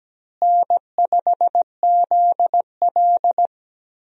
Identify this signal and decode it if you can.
Morse code, heard as a keyed tone, about 17 wpm, 700 Hz. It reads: N5ZL
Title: USA Call Signs - 4 Characters - Repeated 2x - 17wpm